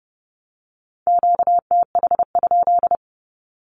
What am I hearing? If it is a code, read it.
Morse code, heard as a keyed tone, about 30 wpm, 690 Hz. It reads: QTH?
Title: Q-Codes with Spoken Meaning - 30wpm